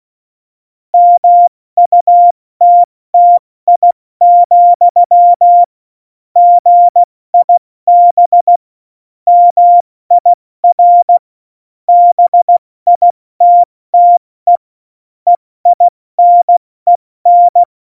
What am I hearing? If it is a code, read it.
Morse code, heard as a keyed tone, about 16 wpm, 700 Hz. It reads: MUTTI, GIB MIR BITTE EINEN
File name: Morsekurs / Witze / Witz 26. September 17, Tempo 70